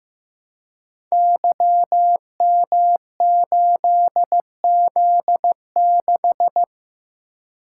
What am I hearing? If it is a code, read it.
Morse code, heard as a keyed tone, about 15 wpm, 700 Hz. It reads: YM8Z6
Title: Code Groups - 5 Letter:Numbers - 15wpm